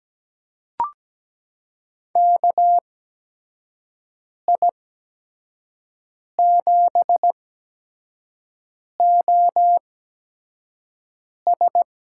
Morse code, heard as a keyed tone, about 17 wpm, spaced as for 5 wpm, 700 Hz. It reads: KI7OS
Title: USA Call Signs - 6 Characters - 8x Character Spacing - 17wpm